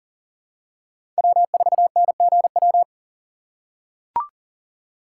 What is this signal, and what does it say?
Morse code, heard as a keyed tone, about 40 wpm, 700 Hz. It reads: W4NGW
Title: USA Call Signs - 4x Character Spacing - 40wpm